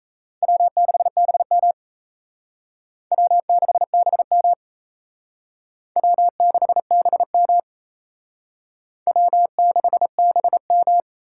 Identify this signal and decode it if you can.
Morse code, heard as a keyed tone, about 36 wpm, 700 Hz. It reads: W6BM W6BM W6BM W6BM